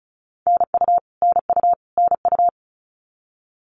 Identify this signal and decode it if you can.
Morse code, heard as a keyed tone, about 34 wpm, 700 Hz. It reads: NU NU NU
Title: Canadian Provinces and Territories - Repeated 3x - 35wpm